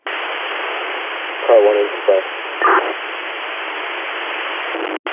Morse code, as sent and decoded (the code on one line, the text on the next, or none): none